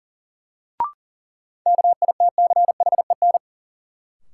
Morse code, heard as a keyed tone, about 40 wpm, 700 Hz. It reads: KITCHEN
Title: Sentences from Top 1400 Words - New Word Drill - 40wpm